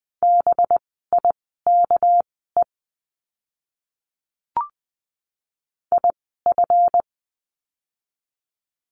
Morse code, heard as a keyed tone, about 20 wpm, spaced as for 15 wpm, 700 Hz. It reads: BIKE IF
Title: 199 Lesson 32f - Letter k - Words 20wpm